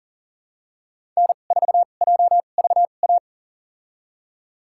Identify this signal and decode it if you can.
Morse code, heard as a keyed tone, about 40 wpm, spaced as for 29 wpm, 700 Hz. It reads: N4JVA
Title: USA Call Signs - 5 Characters - 2x Character Spacing - 40wpm